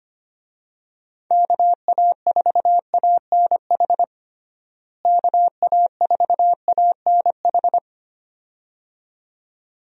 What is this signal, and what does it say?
Morse code, heard as a keyed tone, about 25 wpm, 700 Hz. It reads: KA4ANH KA4ANH
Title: USA Call Signs - 6 Characters - Repeated 2x - 25wpm